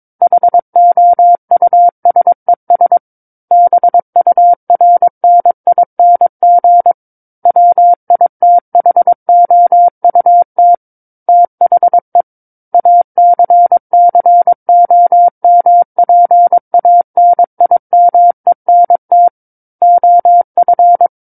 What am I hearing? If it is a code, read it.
Morse code, heard as a keyed tone, about 22 wpm, 700 Hz. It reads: HOUSES BURNING WITHOUT THE ACCOMPANIMENT OF
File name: Book One / 13-Chapter 13 - 22 WPM